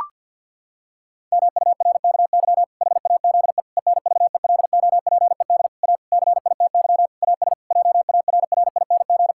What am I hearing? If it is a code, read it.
Morse code, heard as a keyed tone, about 49 wpm, 700 Hz. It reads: MURKY HAZE ENVELOPED A CITY AS JARRING
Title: Pangram Challenge v2 - 50wpm